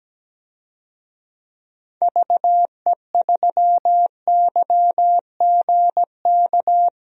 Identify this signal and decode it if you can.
Morse code, heard as a keyed tone, about 17 wpm, 700 Hz. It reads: VE3YGK
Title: DXCC-100 Call Signs - 6 Characters - Repeated 3x - 17wpm